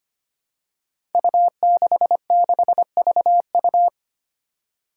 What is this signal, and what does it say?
Morse code, heard as a keyed tone, about 25 wpm, 700 Hz. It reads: U66VU
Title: Five Challenging Letter-Number Combinations 25wpm